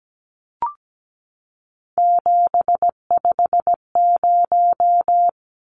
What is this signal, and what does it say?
Morse code, heard as a keyed tone, about 17 wpm, 700 Hz. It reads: 750
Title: Numbers - 3-Digit 17wpm